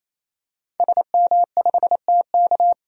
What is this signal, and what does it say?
Morse code, heard as a keyed tone, about 28 wpm, 700 Hz. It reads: SM5TK
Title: DXCC-100 Call Signs - 5 Characters - 28wpm